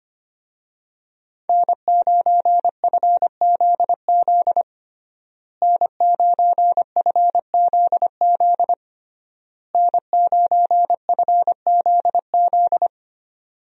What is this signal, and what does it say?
Morse code, heard as a keyed tone, about 25 wpm, 700 Hz. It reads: N9FZZ N9FZZ N9FZZ
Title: USA Call Signs - 5 Characters - Repeated 3x - 25wpm